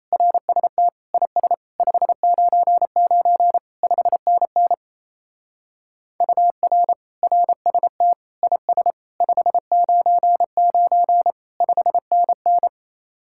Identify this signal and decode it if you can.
Morse code, heard as a keyed tone, about 34 wpm, 700 Hz. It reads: RST IS 599 5NN UR RST IS 599 5NN
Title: Speed-Racing Common QSO Phrases Encore - 28wpm